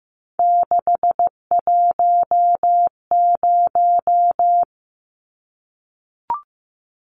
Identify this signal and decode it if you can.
Morse code, heard as a keyed tone, about 15 wpm, 700 Hz. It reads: 610
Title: Numbers - 3-Digit 15wpm